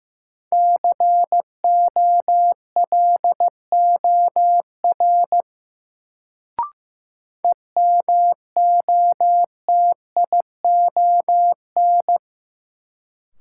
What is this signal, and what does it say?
Morse code, heard as a keyed tone, about 15 wpm, 700 Hz. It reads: COLOR EMOTION
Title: Sentences from Top 1600 Words - New Word Drill Alt - 15wpm